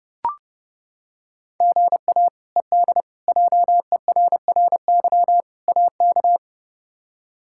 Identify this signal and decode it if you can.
Morse code, heard as a keyed tone, about 31 wpm, 700 Hz. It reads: GA ED JERRY AK
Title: K1USN SST QSO Practice - 30wpm